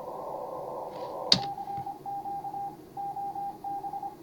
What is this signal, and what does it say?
Morse code, heard as a keyed tone, about 25 wpm, 810 Hz, with noise. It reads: CQ OB
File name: CW / HA5MA RHOB HANG CW 2017